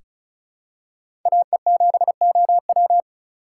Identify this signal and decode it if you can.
Morse code, heard as a keyed tone, about 34 wpm, 700 Hz. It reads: AE7OW
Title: USA Call Signs - 4x Character Spacing - 35wpm